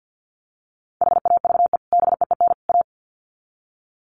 Morse code, heard as a keyed tone, about 50 wpm, 700 Hz. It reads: HAVE BEEN A
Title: Sets of 3 Words 50wpm